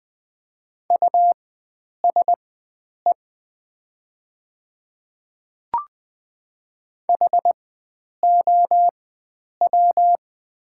Morse code, heard as a keyed tone, about 20 wpm, spaced as for 9 wpm, 690 Hz. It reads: USE HOW
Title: 4x Character Spacing - Top 500 Words - 20wpm